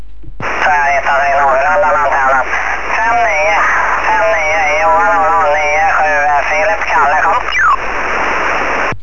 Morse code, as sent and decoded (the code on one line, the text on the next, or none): none